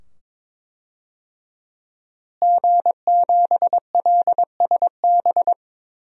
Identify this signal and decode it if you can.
Morse code, heard as a keyed tone, about 22 wpm, 700 Hz. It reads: G7LSB
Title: DXCC-100 Call Signs - 5 Characters - Repeated 2x - 22wpm